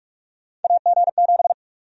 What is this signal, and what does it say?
Morse code, heard as a keyed tone, about 46 wpm, 700 Hz. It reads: AG7
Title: USA Call Sign Prefixes - 45wpm